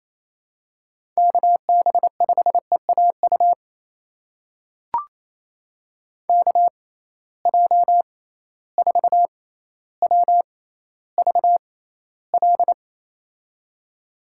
Morse code, heard as a keyed tone, about 28 wpm, spaced as for 14 wpm, 700 Hz. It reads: KB5EAU KJ4WVL
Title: USA Call Signs - 6 Characters - 6x Character Spacing - 28wpm